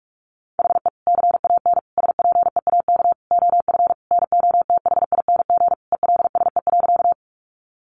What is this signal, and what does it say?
Morse code, heard as a keyed tone, about 44 wpm, 700 Hz. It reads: HE CAN SPEAK OF NOTHING ELSE.
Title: Sentences from Top 1100 Words 45wpm